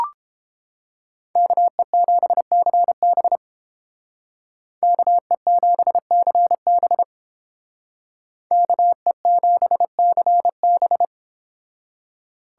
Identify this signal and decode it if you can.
Morse code, heard as a keyed tone, about 31 wpm, 700 Hz. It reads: KE7CB KE7CB KE7CB